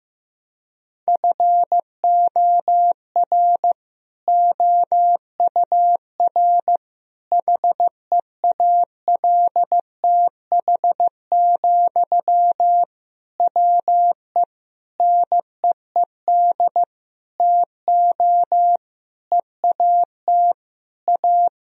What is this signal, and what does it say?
Morse code, heard as a keyed tone, about 15 wpm, 700 Hz. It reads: FOR OUR HEALTH, WE NEED TO EAT A
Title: Sentences from Top 1000 Words - Review Part 4 - 15wpm